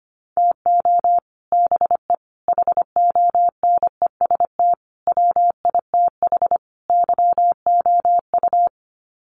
Morse code, heard as a keyed tone, about 25 wpm, 700 Hz. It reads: TO BE HONEST WITH YOU
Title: Sets of 5 Words Encore 25wpm